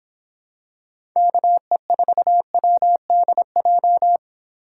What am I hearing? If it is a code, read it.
Morse code, heard as a keyed tone, about 25 wpm, 700 Hz. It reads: KE4WDJ